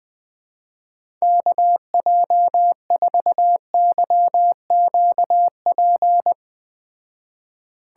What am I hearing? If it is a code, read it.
Morse code, heard as a keyed tone, about 20 wpm, 700 Hz. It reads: KJ4YQP
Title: Speed Racing - USA 2x3 Call Signs 17wpm